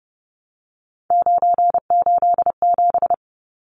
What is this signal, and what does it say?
Morse code, heard as a keyed tone, about 30 wpm, 700 Hz. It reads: 987
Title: Numbers - 3-Digit 30wpm